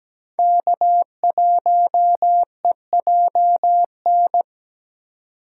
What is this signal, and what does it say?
Morse code, heard as a keyed tone, about 17 wpm, 700 Hz. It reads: K1EJN
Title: USA Call Signs - 5 Characters - 4x Character Spacing - 17wpm